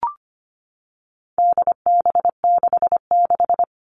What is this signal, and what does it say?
Morse code, heard as a keyed tone, about 25 wpm, 700 Hz. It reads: DB66
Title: All Cut Numbers - 4-Digit 25wpm